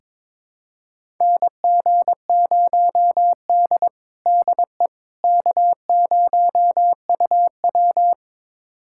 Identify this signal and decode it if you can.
Morse code, heard as a keyed tone, about 22 wpm, 700 Hz. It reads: NG0D DE K0UW
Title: USA Callsign DE Callsign - 4 Characters - 22wpm